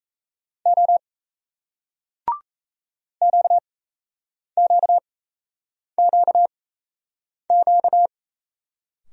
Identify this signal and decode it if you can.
Morse code, heard as a keyed tone, about 35 wpm, spaced as for 8 wpm, 700 Hz. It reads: O QQQQ
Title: Speed-Racing Single Letters 28wpm